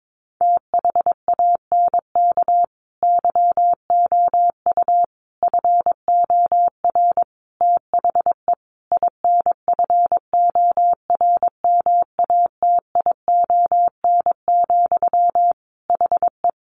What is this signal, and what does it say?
Morse code, heard as a keyed tone, about 22 wpm, 700 Hz. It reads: THANK YOU FOR THE INFORMATION, HE
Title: Sentences from Top 600 Words 22wpm